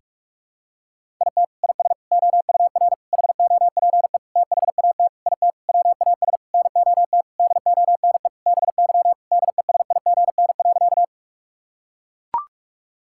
Morse code, heard as a keyed tone, about 42 wpm, 700 Hz. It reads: IT IS OUR HOPE THAT IT WAS NOT DONE BY DESIGN.
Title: Sentences from Top 1000 Words - Review Part 5 - 45wpm